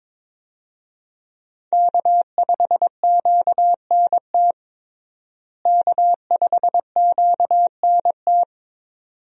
Morse code, heard as a keyed tone, about 22 wpm, 700 Hz. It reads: K5QNT K5QNT
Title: USA Call Signs - 5 Characters - Repeated 2x - 22wpm